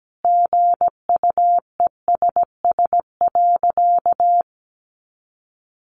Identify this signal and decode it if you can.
Morse code, heard as a keyed tone, about 17 wpm, 700 Hz. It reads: GUESS.
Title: Sentences from Top 1500 Words 17wpm